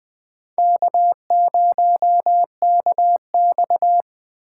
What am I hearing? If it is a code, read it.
Morse code, heard as a keyed tone, about 20 wpm, 700 Hz. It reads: K0KX
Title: K1USN SST QSO Practice - 20wpm